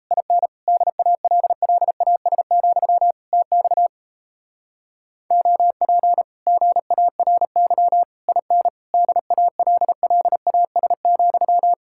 Morse code, heard as a keyed tone, about 38 wpm, 700 Hz. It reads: IN DALLAS, TX OP GARY IN DALLAS,